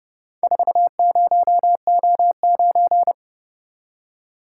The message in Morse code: 40O9

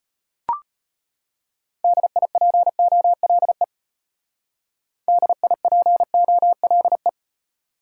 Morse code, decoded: DIPOLE DIPOLE

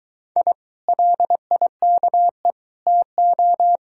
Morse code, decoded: I LIKE TO

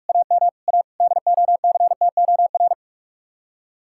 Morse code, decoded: AM A DOCTOR

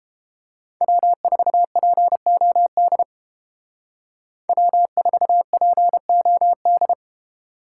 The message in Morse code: W4POD W4POD

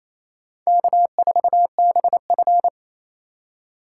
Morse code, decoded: K4BF

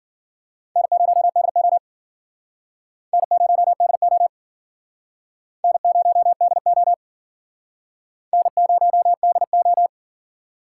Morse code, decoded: N0DO N0DO N0DO N0DO

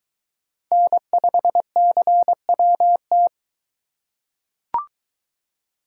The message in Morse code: N5CWT

